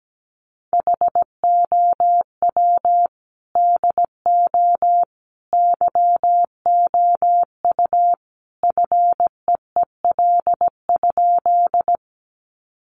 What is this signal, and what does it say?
Morse code, heard as a keyed tone, about 17 wpm, 700 Hz. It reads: HOW DO YOU FEEL?
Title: Sentences from Top 400 Words 17wpm